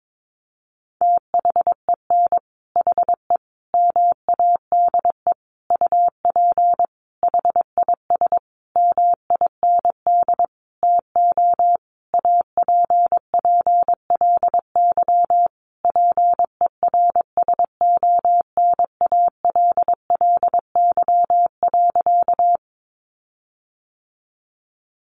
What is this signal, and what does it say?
Morse code, heard as a keyed tone, about 22 wpm, 700 Hz. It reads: THEN HE MADE UP HIS MIND TO APPLY PERSONALLY.
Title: Sentences from Top 1400 Words 22wpm